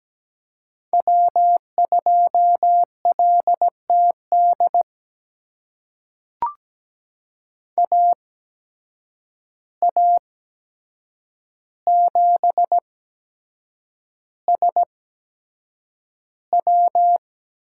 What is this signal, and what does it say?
Morse code, heard as a keyed tone, about 17 wpm, spaced as for 7 wpm, 700 Hz. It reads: W2LTD AA7SW
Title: USA Call Signs - 5 Characters - 8x Character Spacing - 17wpm